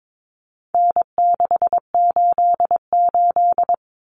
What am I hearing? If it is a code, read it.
Morse code, heard as a keyed tone, about 22 wpm, 700 Hz. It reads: N688
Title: Common Cut Numbers - 4-Digit 22wpm